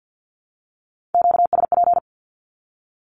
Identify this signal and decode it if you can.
Morse code, heard as a keyed tone, about 48 wpm, 700 Hz. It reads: QSP
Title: Q-Codes 50wpm